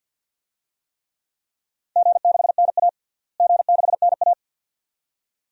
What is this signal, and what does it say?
Morse code, heard as a keyed tone, about 52 wpm, 700 Hz. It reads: G6NA G6NA
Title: DXCC-100 Call Signs - 4 Characters - Repeated 2x - 50wpm